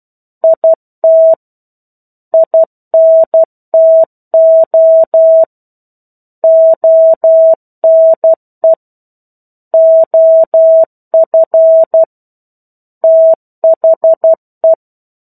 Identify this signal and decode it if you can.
Morse code, heard as a keyed tone, about 12 wpm, 650 Hz. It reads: IT INTO ONE OF THE
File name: 01-Chapter-01 12 WPM